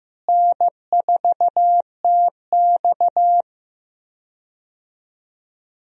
Morse code, heard as a keyed tone, about 16 wpm, 700 Hz. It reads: N4TX
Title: USA Call Signs - 4 Characters - Repeated 3x - 15wpm